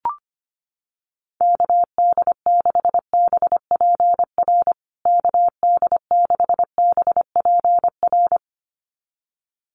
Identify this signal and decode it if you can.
Morse code, heard as a keyed tone, about 25 wpm, 700 Hz. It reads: KD6BPR KD6BPR
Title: USA Call Signs Repeated 2x 25wpm